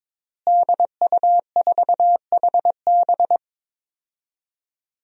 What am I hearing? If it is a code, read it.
Morse code, heard as a keyed tone, about 22 wpm, 700 Hz. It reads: DU4HB
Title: DXCC-100 Call Signs - 5 Characters - Repeated 3x - 22wpm